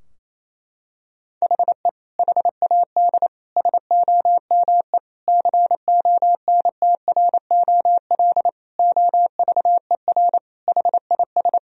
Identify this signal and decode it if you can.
Morse code, heard as a keyed tone, about 28 wpm, 700 Hz. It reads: HE HAD SOME CONTROL OVER HIS
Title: Sentences from Top 1000 Words - Review Part 3 - 28wpm